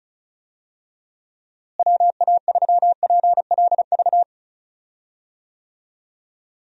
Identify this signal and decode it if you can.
Morse code, heard as a keyed tone, about 34 wpm, 700 Hz. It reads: WA3PLV
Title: USA Call Signs - 6 Characters - Repeated 2x - 35wpm